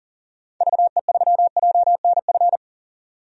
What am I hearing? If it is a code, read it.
Morse code, heard as a keyed tone, about 40 wpm, 700 Hz. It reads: VE3JNF
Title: DXCC-100 Call Signs - 6 Characters - 40wpm